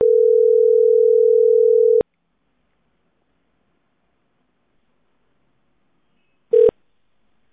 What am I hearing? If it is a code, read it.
Ringback tone (North America), 440 + 480 Hz.